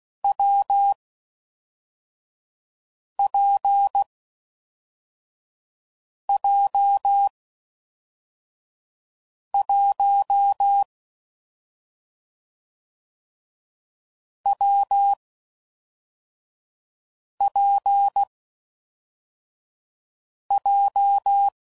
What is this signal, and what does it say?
Morse code, heard as a keyed tone, about 16 wpm, 800 Hz. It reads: W P J 1 W P J
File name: Combinations / Difficult Combos / AWJ1